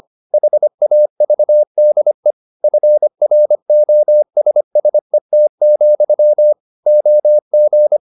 Morse code, heard as a keyed tone, about 25 wpm, 600 Hz. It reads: HAVDE FROSSET, OG
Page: Ferie 25 wpm